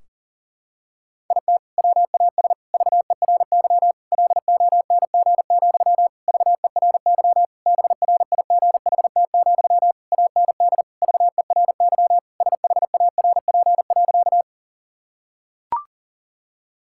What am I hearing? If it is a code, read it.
Morse code, heard as a keyed tone, about 40 wpm, 700 Hz. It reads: IT WAS VERY LONG, VERY BRIGHT, AND VERY SHARP.
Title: Sentences from Top 1600 Words 40wpm